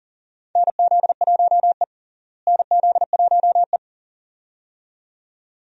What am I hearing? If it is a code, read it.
Morse code, heard as a keyed tone, about 40 wpm, 700 Hz. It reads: NZ1E NZ1E
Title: USA Call Signs - 4 Characters - Repeated 2x - 40wpm